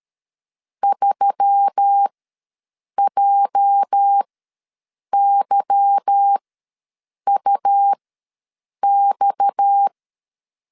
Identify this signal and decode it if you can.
Morse code, heard as a keyed tone, about 13 wpm, spaced as for 7 wpm, 800 Hz. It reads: 3JYUX
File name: LEZIONI / 10 SIGNS / 10A SLASH ws 12wpm 3443